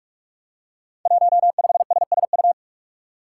Morse code, heard as a keyed tone, about 44 wpm, 700 Hz. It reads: 15SSU
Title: Five Challenging Letter-Number Combinations 45wpm